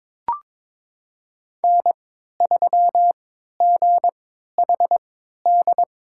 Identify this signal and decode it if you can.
Morse code, heard as a keyed tone, about 22 wpm, spaced as for 12 wpm, 700 Hz. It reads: N3GHD
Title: USA Call Signs - 5 Characters - 3x Character Spacing - 22wpm